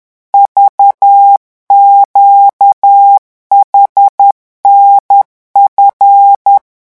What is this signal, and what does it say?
Morse code, heard as a keyed tone, about 10 wpm, 800 Hz. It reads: VQHNF
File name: morse mp3 / revision8-vitesse 10 mots